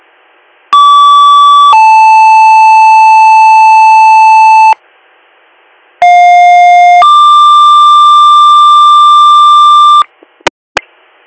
Two-tone page, heard as a one-second tone, then a three-second tone, about 1150 then 880 Hz, about 710 then 1180 Hz. It is heard from 0.7 to 10.0 s.